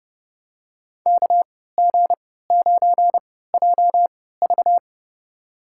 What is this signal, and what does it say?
Morse code, heard as a keyed tone, about 29 wpm, spaced as for 17 wpm, 700 Hz. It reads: KG9JV
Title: USA Call Signs - 3x Character Spacing - 30wpm